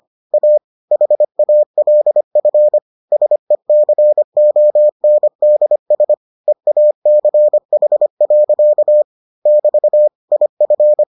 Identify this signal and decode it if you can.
Morse code, heard as a keyed tone, about 25 wpm, 600 Hz. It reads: A HALF SECONDS EACH. <BT> IF